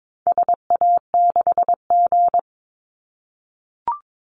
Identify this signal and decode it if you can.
Morse code, heard as a keyed tone, about 22 wpm, 700 Hz. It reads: SA6G
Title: Call Signs - 22wpm